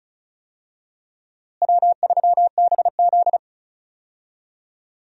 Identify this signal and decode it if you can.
Morse code, heard as a keyed tone, about 36 wpm, 700 Hz. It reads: W3BZ
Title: USA Call Signs - 4 Characters - Repeated 3x - 35wpm